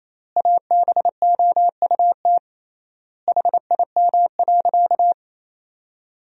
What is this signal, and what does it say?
Morse code, heard as a keyed tone, about 28 wpm, 700 Hz. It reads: ABOUT HIM.
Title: 3x Word-Spacing - Sentences from Top 200 Words - 28wpm